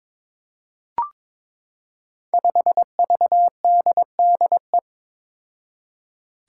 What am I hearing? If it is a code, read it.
Morse code, heard as a keyed tone, about 21 wpm, 700 Hz. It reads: 5VDDE
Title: Five Challenging Letter-Number Combinations 22wpm